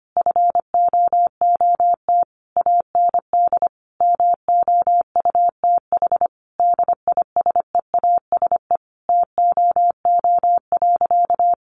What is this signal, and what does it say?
Morse code, heard as a keyed tone, about 25 wpm, 700 Hz. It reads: FOOT AND MOUTH DISEASE TOO.
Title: Sentences from Top 1500 Words 25wpm